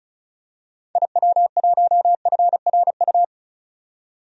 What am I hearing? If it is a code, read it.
Morse code, heard as a keyed tone, about 36 wpm, 700 Hz. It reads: IW1FRU
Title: DXCC-100 Call Signs - 6 Characters - 35wpm